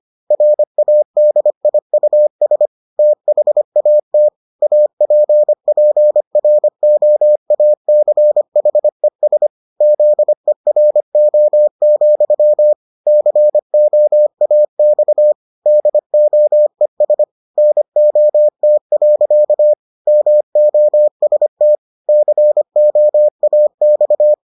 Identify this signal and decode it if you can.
Morse code, heard as a keyed tone, about 25 wpm, 600 Hz. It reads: RADIUS THAT APPROACHES ZERO, COAX DOES NOT. MOST COAX